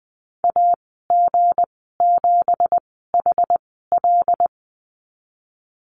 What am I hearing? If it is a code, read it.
Morse code, heard as a keyed tone, about 19 wpm, spaced as for 14 wpm, 700 Hz. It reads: AG7HL
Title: 167 Lesson 27g - Number 7 - Call Signs 20wpm